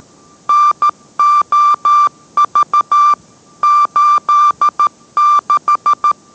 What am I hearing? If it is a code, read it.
Morse code, heard as a keyed tone, about 13 wpm, 1180 Hz. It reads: NOV 86